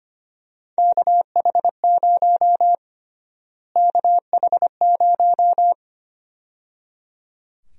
Morse code, heard as a keyed tone, about 25 wpm, 700 Hz. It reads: KH0 KH0
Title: USA Call Sign Prefixes - Repeated 2x - 25wpm